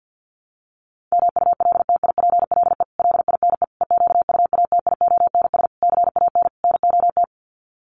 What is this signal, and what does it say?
Morse code, heard as a keyed tone, about 50 wpm, 700 Hz. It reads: MULTIPLE LINE EQUATIONS CAN NOT
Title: Sets of 5 Words 50wpm